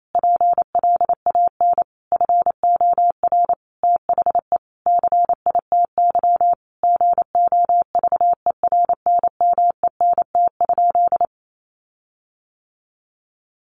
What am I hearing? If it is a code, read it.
Morse code, heard as a keyed tone, about 28 wpm, 700 Hz. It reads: PLAN FOR THE CITY GOVERNMENT?
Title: Sentences from Top 1400 Words 28wpm